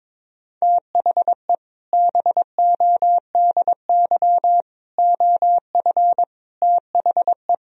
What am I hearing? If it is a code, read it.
Morse code, heard as a keyed tone, about 22 wpm, 700 Hz. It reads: THE BODY OF THE
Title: Sentences from Top 1000 Words - Review Part 2 - 22wpm